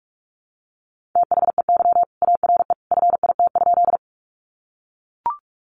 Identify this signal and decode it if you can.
Morse code, heard as a keyed tone, about 44 wpm, 700 Hz. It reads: THEY ARE FIT?
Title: Sentences from Top 800 Words 45wpm